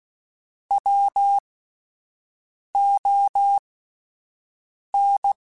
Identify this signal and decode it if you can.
Morse code, heard as a keyed tone, about 16 wpm, spaced as for 5 wpm, 800 Hz. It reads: WON
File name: Alphabetical / Alphabetical mp3s / Ws @5